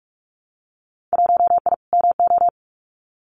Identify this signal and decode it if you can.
Morse code, heard as a keyed tone, about 46 wpm, 700 Hz. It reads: 1I MO
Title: Winter Field Day Exchanges Encore 45wpm